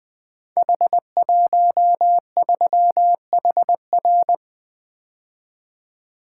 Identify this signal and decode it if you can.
Morse code, heard as a keyed tone, about 20 wpm, 700 Hz. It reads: H13HR
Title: Code Groups - 5 Letter:Numbers - 20wpm